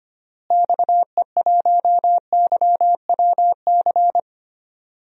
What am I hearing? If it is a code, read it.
Morse code, heard as a keyed tone, about 25 wpm, 700 Hz. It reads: XE1YWC
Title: DXCC-100 Call Signs - 6 Characters - Repeated 3x - 25wpm